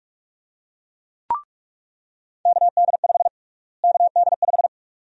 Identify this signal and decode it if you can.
Morse code, heard as a keyed tone, about 42 wpm, 700 Hz. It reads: KD5 KD5
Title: USA Call Sign Prefixes - Repeated 2x - 45wpm